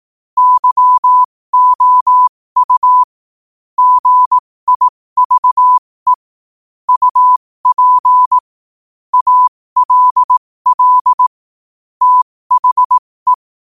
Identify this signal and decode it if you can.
Morse code, heard as a keyed tone, about 18 wpm, spaced as for 15 wpm, 1000 Hz. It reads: YOU GIVE UP ALL THE
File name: qotdfiles / 15wpm